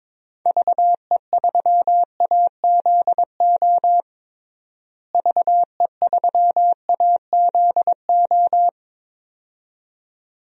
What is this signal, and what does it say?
Morse code, heard as a keyed tone, about 22 wpm, 700 Hz. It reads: VE3AZO VE3AZO
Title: DXCC-100 Call Signs - 6 Characters - Repeated 2x - 22wpm